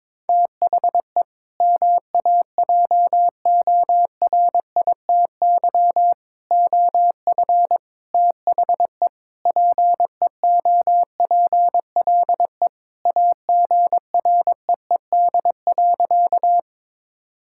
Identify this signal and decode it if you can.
Morse code, heard as a keyed tone, about 22 wpm, 700 Hz. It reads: THE MAJORITY OF THE PEOPLE AGREED.
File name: practice / 22wpm / Sentences with 4-Word Phrases - 22wpm